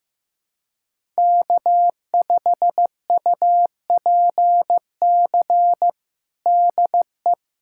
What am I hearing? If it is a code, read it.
Morse code, heard as a keyed tone, about 16 wpm, 700 Hz. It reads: K5UPC DE
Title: USA Callsign DE Callsign 15wpm